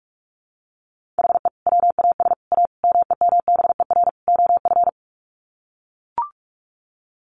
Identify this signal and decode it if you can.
Morse code, heard as a keyed tone, about 44 wpm, 700 Hz. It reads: HE WAS A MEMBER OF
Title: Sets of 5 Words Encore2 45wpm